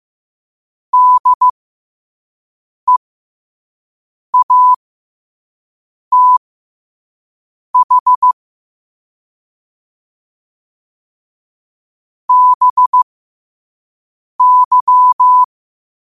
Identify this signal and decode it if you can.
Morse code, heard as a keyed tone, about 15 wpm, spaced as for 5 wpm, 1000 Hz. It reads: DEATH BY
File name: qotdfiles / 5wpm